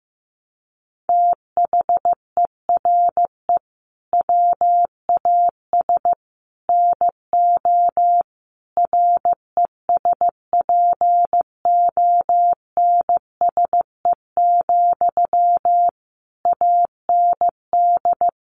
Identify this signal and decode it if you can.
Morse code, heard as a keyed tone, about 14 wpm, 700 Hz. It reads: THERE WAS NO RESPONSE, AND
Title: Sentences from Top 1400 Words 15wpm